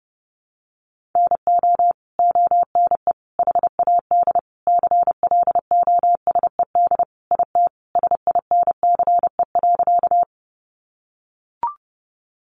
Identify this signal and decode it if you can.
Morse code, heard as a keyed tone, about 30 wpm, 700 Hz. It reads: NO ONE HAD CLOSED IT SINCE.
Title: Sentences from Top 1400 Words 30wpm